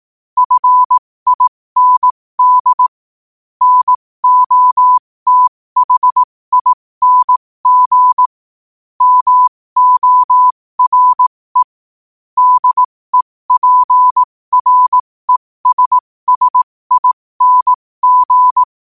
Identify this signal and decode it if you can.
Morse code, heard as a keyed tone, about 18 wpm, spaced as for 15 wpm, 1000 Hz. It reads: FIND NOTHING MORE DEPRESSING
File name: qotdfiles / 15wpm